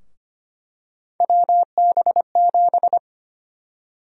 Morse code, heard as a keyed tone, about 25 wpm, 700 Hz. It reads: WB7
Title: USA Call Sign Prefixes - Repeated 2x - 25wpm